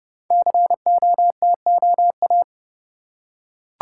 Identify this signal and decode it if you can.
Morse code, heard as a keyed tone, about 30 wpm, 700 Hz. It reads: COTOA
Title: Code Groups - 5 Letters - 30wpm